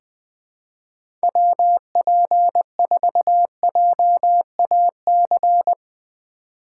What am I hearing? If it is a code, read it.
Morse code, heard as a keyed tone, about 21 wpm, 700 Hz. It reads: WP4JAC
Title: USA Call Signs - 3x Character Spacing - 20wpm